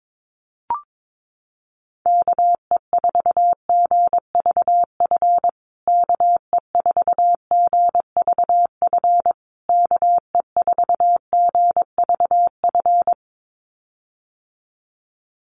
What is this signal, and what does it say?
Morse code, heard as a keyed tone, about 22 wpm, 700 Hz. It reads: KE4GVF KE4GVF KE4GVF
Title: USA Call Signs Repeated 3x 22wpm